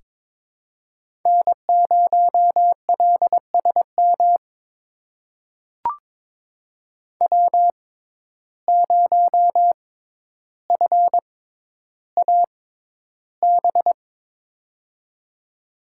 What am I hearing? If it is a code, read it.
Morse code, heard as a keyed tone, about 22 wpm, spaced as for 11 wpm, 700 Hz. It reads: N0LSM W0FAB
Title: USA Call Signs - 5 Characters - 6x Character Spacing - 22wpm